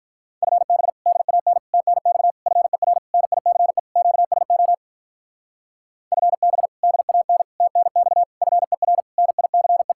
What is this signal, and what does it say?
Morse code, heard as a keyed tone, about 43 wpm, 700 Hz. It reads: FB DAN TNX FER NICE QSO FB DAN TNX FER NICE
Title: Speed-Racing Common QSO Phrases Encore - 35wpm